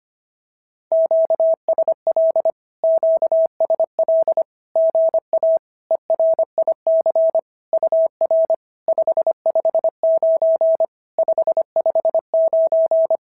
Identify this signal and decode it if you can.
Morse code, heard as a keyed tone, about 25 wpm, 650 Hz. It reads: QSL QSL GA ERIC UR 559 559
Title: SOTA QSO Practice 25wpm